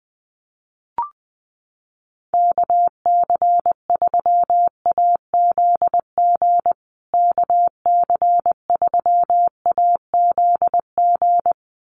Morse code, heard as a keyed tone, about 20 wpm, 700 Hz. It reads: KC3AZG KC3AZG
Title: USA Call Signs Repeated 2x 20wpm